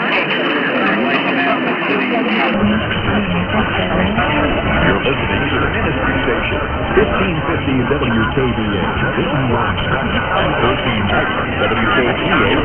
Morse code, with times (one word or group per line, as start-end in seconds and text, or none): none